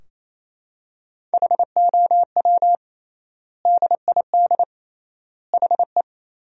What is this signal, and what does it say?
Morse code, heard as a keyed tone, about 28 wpm, 700 Hz. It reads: HOW DID HE
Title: 3x Word-Spacing - Sentences from Top 100 Words - 28wpm